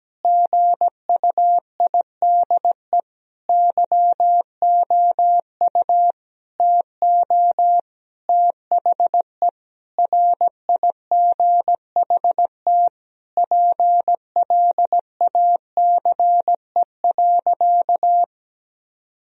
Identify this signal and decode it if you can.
Morse code, heard as a keyed tone, about 17 wpm, 700 Hz. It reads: GUIDE YOU TO THE RIGHT PLACE.
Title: Sentences from Top 1000 Words - Review Part 3 - 17wpm